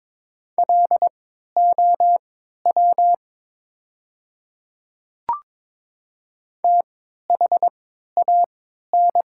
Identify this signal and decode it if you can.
Morse code, heard as a keyed tone, about 22 wpm, spaced as for 12 wpm, 700 Hz. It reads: LOW THAN
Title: 3x Character Spacing - Top 300 Words - 22wpm